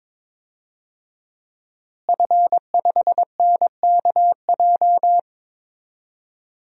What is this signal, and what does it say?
Morse code, heard as a keyed tone, about 22 wpm, 700 Hz. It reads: F5NKJ
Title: DXCC-100 Call Signs - 5 Characters - Repeated 3x - 22wpm